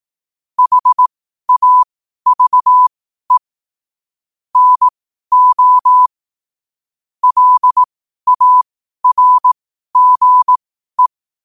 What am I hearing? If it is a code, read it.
Morse code, heard as a keyed tone, about 18 wpm, spaced as for 12 wpm, 1000 Hz. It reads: HAVE NO LARGE
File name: qotdfiles / 12wpm